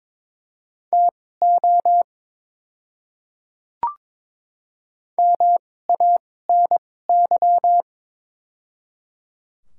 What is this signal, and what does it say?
Morse code, heard as a keyed tone, about 21 wpm, spaced as for 16 wpm, 700 Hz. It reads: TO MANY
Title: 2x Character Spacing - Top 100 Words - 22wpm